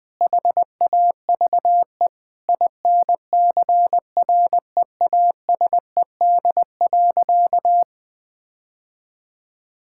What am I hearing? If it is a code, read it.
Morse code, heard as a keyed tone, about 20 wpm, 700 Hz. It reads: HAVE INCREASED.
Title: Sentences with 6-Word Phrases - 20wpm